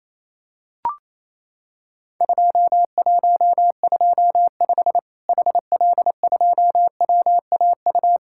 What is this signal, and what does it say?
Morse code, heard as a keyed tone, about 28 wpm, 700 Hz. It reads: 2125 HL2WAU
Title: Worked All Europe DX Contest - QTC Exchange - 28wpm